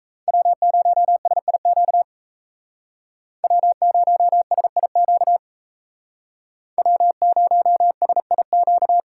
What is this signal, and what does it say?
Morse code, heard as a keyed tone, about 35 wpm, 700 Hz. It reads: W0SIQ W0SIQ W0SIQ
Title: Speed Racing - USA 1x3 Call Signs 28wpm